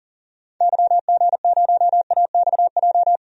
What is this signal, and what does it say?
Morse code, heard as a keyed tone, about 40 wpm, 700 Hz. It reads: YG0AXJ
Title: DXCC-100 Call Signs - 6 Characters - Repeated 3x - 40wpm